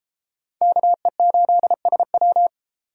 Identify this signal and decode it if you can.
Morse code, heard as a keyed tone, about 33 wpm, 700 Hz. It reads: KE8SW